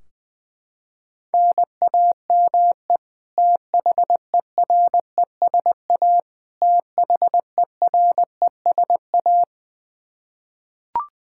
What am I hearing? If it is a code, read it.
Morse code, heard as a keyed tone, about 20 wpm, 700 Hz. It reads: NAME THERESA THERESA
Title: Practice QSO - Exchange 1 - Beginning - 20wpm